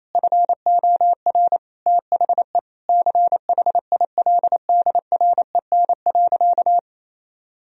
Morse code, heard as a keyed tone, about 28 wpm, 700 Hz. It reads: FOR THE CHILDREN.
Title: Sentences from Top 1000 Words - Review Part 2 - 28wpm